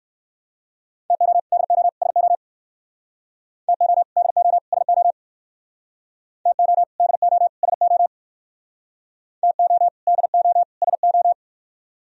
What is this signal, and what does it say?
Morse code, heard as a keyed tone, about 57 wpm, 700 Hz. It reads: TO DO SO TO DO SO TO DO SO TO DO SO